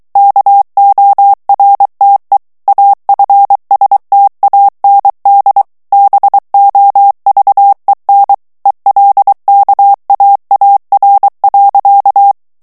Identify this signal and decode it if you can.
Morse code, heard as a keyed tone, about 23 wpm, 800 Hz. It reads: KORTE AFSTAND BOVEN ELKAAR.
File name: HOLLANDAIS texte clair 1 vitesse 22 mots minute